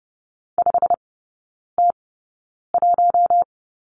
Morse code, heard as a keyed tone, about 30 wpm, spaced as for 9 wpm, 700 Hz. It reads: 5T1
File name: course / 30:10wpm / 087 Lesson 15e - Number 5 - Three Characters 30wpm